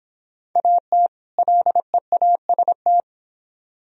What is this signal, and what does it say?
Morse code, heard as a keyed tone, about 26 wpm, 700 Hz. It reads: AT LEAST